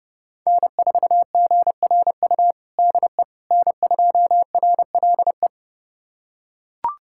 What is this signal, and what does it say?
Morse code, heard as a keyed tone, about 32 wpm, 700 Hz. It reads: N4GRU DE N2RLE
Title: USA Callsign DE Callsign - 5 Characters - 30wpm